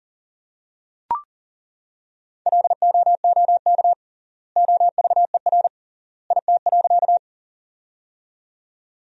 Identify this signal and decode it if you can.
Morse code, heard as a keyed tone, about 40 wpm, 700 Hz. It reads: LOOK OVER IT.
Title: 3x Word-Spacing - Sentences from Top 100 Words - 40wpm